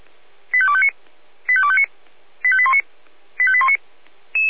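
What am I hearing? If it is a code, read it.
Five-tone selective call (ZVEI), heard as quick runs of five tones, tones of about 70 ms. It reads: 85259 85259 85819 85819
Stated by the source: ZVEI five-tone selective call received on SDR.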